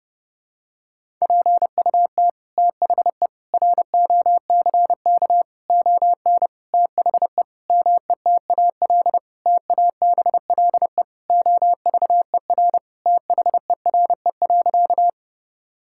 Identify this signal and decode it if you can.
Morse code, heard as a keyed tone, about 32 wpm, 700 Hz. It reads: PUT THE ROCK ON THE METAL TABLE OVER THERE.
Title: Sentences from Top 1000 Words - Review Part 6 - 30wpm